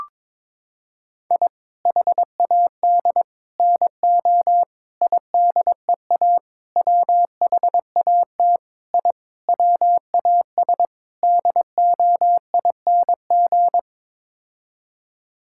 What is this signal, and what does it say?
Morse code, heard as a keyed tone, about 22 wpm, 700 Hz. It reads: I HAD NO IDEA WHAT I WAS DOING
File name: practice / 22wpm / Sets of 8 Words - 22wpm